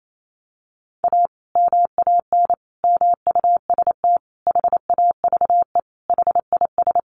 Morse code, heard as a keyed tone, about 28 wpm, 700 Hz. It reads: A MAN MUST HAVE HIS
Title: Sentences from Top 1200 Words 28wpm